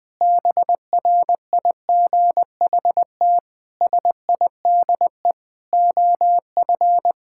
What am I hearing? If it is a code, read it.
Morse code, heard as a keyed tone, about 20 wpm, 700 Hz. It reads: BRIGHT SIDE OF
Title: Sentences with 8-Word Phrases - Spoken Beginning - 20wpm